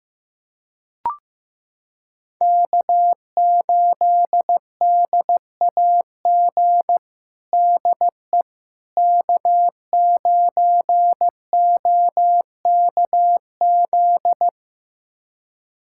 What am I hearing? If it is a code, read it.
Morse code, heard as a keyed tone, about 15 wpm, 700 Hz. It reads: K8DAG DE K9OKZ
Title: USA Callsign DE Callsign - 5 Characters - 15wpm